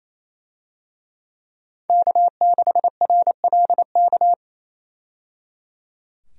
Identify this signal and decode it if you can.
Morse code, heard as a keyed tone, about 28 wpm, 700 Hz. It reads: K6RLK
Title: USA Call Signs - 5 Characters - 28wpm